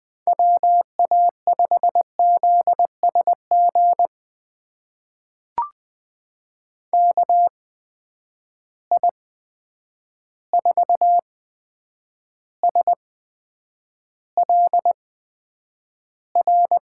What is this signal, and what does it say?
Morse code, heard as a keyed tone, about 20 wpm, spaced as for 9 wpm, 700 Hz. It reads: WA5ZSG KI4SLR
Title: USA Call Signs - 6 Characters - 8x Character Spacing - 20wpm